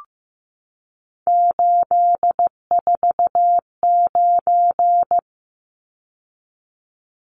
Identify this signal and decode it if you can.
Morse code, heard as a keyed tone, about 15 wpm, 700 Hz. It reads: 849
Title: Numbers - 3-Digit 15wpm